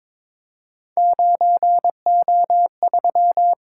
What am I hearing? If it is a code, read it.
Morse code, heard as a keyed tone, about 22 wpm, 700 Hz. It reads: 9O3
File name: practice / 22wpm / Three Letter-Number Combinations - 22wpm